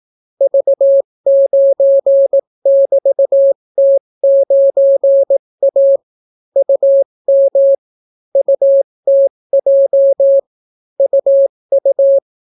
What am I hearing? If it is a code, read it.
Morse code, heard as a keyed tone, about 18 wpm, 550 Hz. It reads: V9<BT>T9A UM UTJ UU